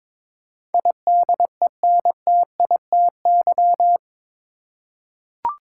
Morse code, heard as a keyed tone, about 22 wpm, 700 Hz. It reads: IDENTITY
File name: practice / 22wpm / Eight Letter Words - 22wpm